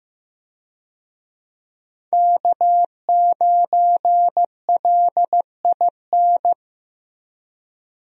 Morse code, heard as a keyed tone, about 15 wpm, 700 Hz. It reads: K9LIN
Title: USA Call Signs - 5 Characters - Repeated 3x - 15wpm